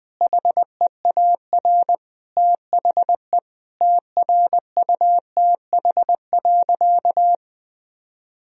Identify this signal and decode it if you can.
Morse code, heard as a keyed tone, about 20 wpm, 700 Hz. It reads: HEAR THE TRUTH.
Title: Sentences from Top 1000 Words - Review Part 1 - 20wpm